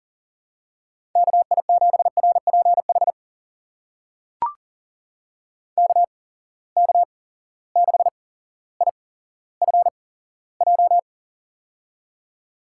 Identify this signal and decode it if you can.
Morse code, heard as a keyed tone, about 40 wpm, spaced as for 17 wpm, 700 Hz. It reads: KI7RPH KK6IFJ
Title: USA Call Signs - 6 Characters - 8x Character Spacing - 40wpm